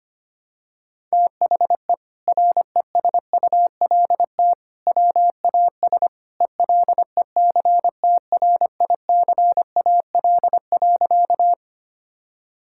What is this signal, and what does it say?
Morse code, heard as a keyed tone, about 25 wpm, 700 Hz. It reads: THE RESULT WAS ELECTRICAL.
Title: Sentences from Top 1000 Words - Review Part 6 - 25wpm